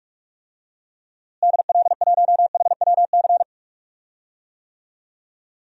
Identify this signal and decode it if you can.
Morse code, heard as a keyed tone, about 43 wpm, 700 Hz. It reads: DL1HWC
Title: DXCC-100 Call Signs - 6 Characters - Repeated 3x - 45wpm